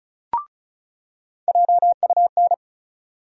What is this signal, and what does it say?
Morse code, heard as a keyed tone, about 36 wpm, 700 Hz. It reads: JUN
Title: USA Call Sign Postfixes - 3 Letters 35wpm